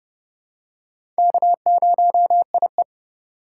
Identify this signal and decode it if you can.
Morse code, heard as a keyed tone, about 30 wpm, 700 Hz. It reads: K0IE
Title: USA Call Signs - 4 Characters - 2x Character Spacing - 30wpm